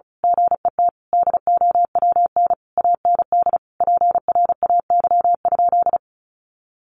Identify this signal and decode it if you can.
Morse code, heard as a keyed tone, about 34 wpm, 700 Hz. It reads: GET DOWN AND PRAY?
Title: A Sound Of Thunder 35wpm